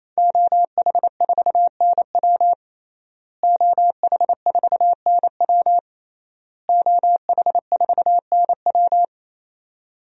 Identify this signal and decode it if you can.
Morse code, heard as a keyed tone, about 28 wpm, 700 Hz. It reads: OH4NW OH4NW OH4NW
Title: DXCC-100 Call Signs - 5 Characters - Repeated 3x - 28wpm